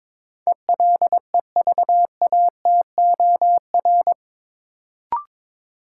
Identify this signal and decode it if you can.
Morse code, heard as a keyed tone, about 22 wpm, 700 Hz. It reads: ELEVATOR
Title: Sentences from Top 1600 Words - New Word Drill Alt - 22wpm